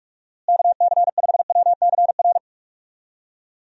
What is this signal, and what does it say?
Morse code, heard as a keyed tone, about 41 wpm, 700 Hz. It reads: KC5WCR
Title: USA Call Signs - 6 Characters - Repeated 2x - 45wpm